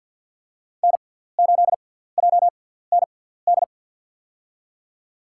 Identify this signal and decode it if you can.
Morse code, heard as a keyed tone, about 52 wpm, spaced as for 17 wpm, 700 Hz. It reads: N8JND
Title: USA Call Signs - 5 Characters - 6x Character Spacing - 50wpm